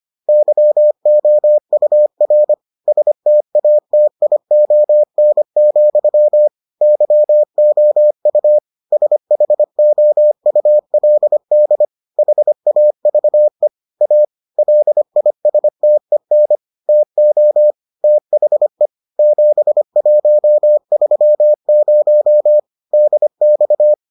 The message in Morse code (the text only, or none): YOUR STATION, YOU SHOULD HAVE A LISTEN TO THE 7130 DX